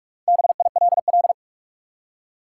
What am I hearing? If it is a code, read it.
Morse code, heard as a keyed tone, about 46 wpm, 700 Hz. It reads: BILL
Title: Four Letter Words - 45wpm